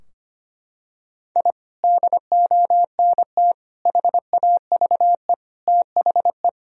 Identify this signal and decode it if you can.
Morse code, heard as a keyed tone, about 25 wpm, 700 Hz. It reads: I DONT HAVE THE
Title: Sentences from Top 1000 Words - Review Part 3 - 25wpm